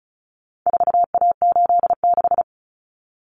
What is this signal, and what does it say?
Morse code, heard as a keyed tone, about 34 wpm, 700 Hz. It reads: 4A86
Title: Common Cut Numbers - 4-Digit 35wpm